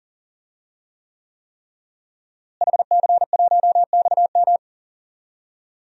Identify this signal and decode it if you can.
Morse code, heard as a keyed tone, about 40 wpm, 700 Hz. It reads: HC1XM
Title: DXCC-100 Call Signs - 5 Characters - Repeated 3x - 40wpm